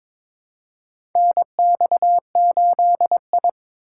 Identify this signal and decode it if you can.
Morse code, heard as a keyed tone, about 22 wpm, 700 Hz. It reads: NX8I
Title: USA Call Signs - 4 Characters - 8x Character Spacing - 22wpm